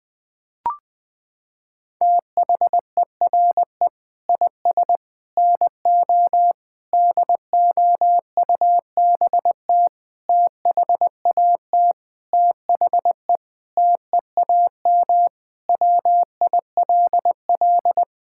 Morse code, heard as a keyed tone, about 21 wpm, 700 Hz. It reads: THERE IS NO DOUBT THAT THE TEAM WILL
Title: Sentences with 6-Word Phrases - 20wpm